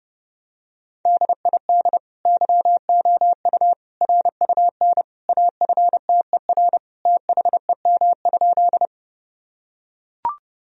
Morse code, heard as a keyed tone, about 30 wpm, 700 Hz. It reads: DID YOU RUN AFTER THEM?
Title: Sentences from Top 1000 Words - Review Part 1 - 30wpm